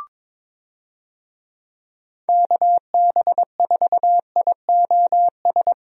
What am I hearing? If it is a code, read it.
Morse code, heard as a keyed tone, about 22 wpm, 700 Hz. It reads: KB4IOS
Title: USA Call Signs - 6 Characters - Repeated 2x - 22wpm